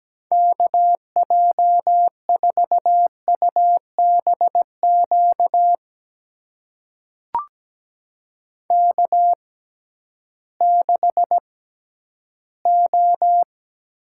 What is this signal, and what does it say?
Morse code, heard as a keyed tone, about 17 wpm, spaced as for 11 wpm, 700 Hz. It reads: KJ4UBQ K6O
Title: USA Call Signs - 6x Character Spacing - 17wpm